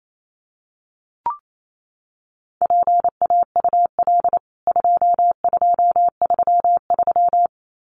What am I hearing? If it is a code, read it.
Morse code, heard as a keyed tone, about 28 wpm, 700 Hz. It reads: PAUL 2233
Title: CWT Exchange 28wpm